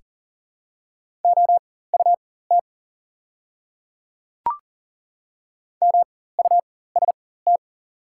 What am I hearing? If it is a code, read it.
Morse code, heard as a keyed tone, about 40 wpm, spaced as for 19 wpm, 690 Hz. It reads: OUT MUST
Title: 4x Character Spacing - Top 200 Words - 40wpm